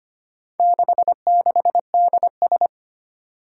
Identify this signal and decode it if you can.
Morse code, heard as a keyed tone, about 25 wpm, 700 Hz. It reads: 66DS
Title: Four Challenging Letter-Number Combinations 25wpm